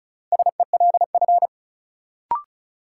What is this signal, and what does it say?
Morse code, heard as a keyed tone, about 36 wpm, 700 Hz. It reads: SELF
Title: Sentences from Top 1000 Words - Review Part 1 - New Word Drill - 35wpm